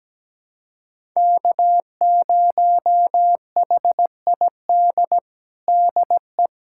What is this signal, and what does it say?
Morse code, heard as a keyed tone, about 17 wpm, 700 Hz. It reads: K0HID DE
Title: USA Callsign DE Callsign - 5 Characters - 17wpm